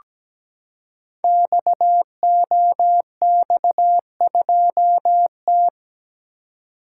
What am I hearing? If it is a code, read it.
Morse code, heard as a keyed tone, about 17 wpm, 700 Hz. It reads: XOX2T
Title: Code Groups - 5 Letter:Numbers - 17wpm